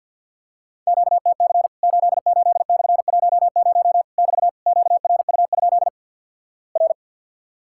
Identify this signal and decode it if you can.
Morse code, heard as a keyed tone, about 50 wpm, 700 Hz. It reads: QTC 88/10 <BT> QRU? R
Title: Worked All Europe DX Contest - QTC Exchange with Repeats - 50wpm